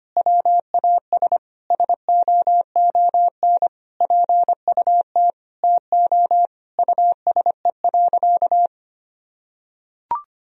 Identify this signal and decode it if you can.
Morse code, heard as a keyed tone, about 25 wpm, 700 Hz. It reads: WAS SOON PUT TO USE.
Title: Sentences from Top 1000 Words - Review Part 3 - 25wpm